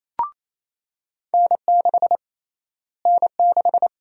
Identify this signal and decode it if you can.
Morse code, heard as a keyed tone, about 28 wpm, 700 Hz. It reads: N6 N6
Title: USA Call Sign Prefixes - Repeated 2x - 28wpm